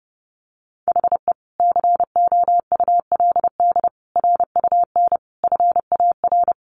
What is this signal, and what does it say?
Morse code, heard as a keyed tone, about 30 wpm, 700 Hz. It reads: HE COULD RUN FAR
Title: Sentences from Top 1100 Words 30wpm